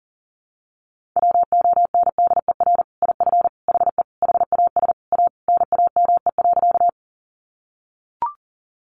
Morse code, heard as a keyed tone, about 40 wpm, 700 Hz. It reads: WONDER IF HE HAS A NAME.
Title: Sentences from Top 500 Words 40wpm